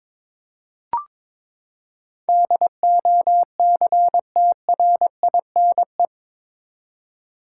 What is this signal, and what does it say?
Morse code, heard as a keyed tone, about 22 wpm, 700 Hz. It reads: DOCTRINE
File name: practice / 22wpm / Eight Letter Words - 22wpm